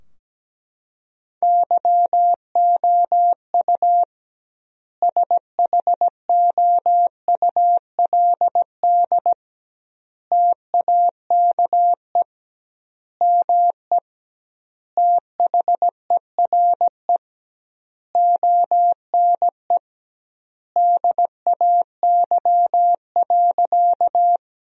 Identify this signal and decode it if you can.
Morse code, heard as a keyed tone, about 17 wpm, 700 Hz. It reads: YOU SHOULD TAKE ME THERE ONE DAY.
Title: 2x Word-Spacing - Sentences from Top 300 Words - 17wpm